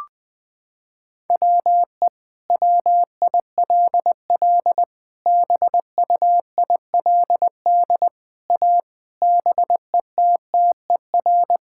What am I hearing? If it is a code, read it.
Morse code, heard as a keyed tone, about 20 wpm, 700 Hz. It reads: WE WILL BUILD A BETTER
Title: Sentences from Top 1000 Words - Review Part 6 - 20wpm